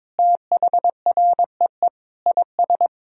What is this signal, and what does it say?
Morse code, heard as a keyed tone, about 22 wpm, 700 Hz. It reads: THREE IS
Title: Sentences from Top 1000 Words - Review Part 1 - 22wpm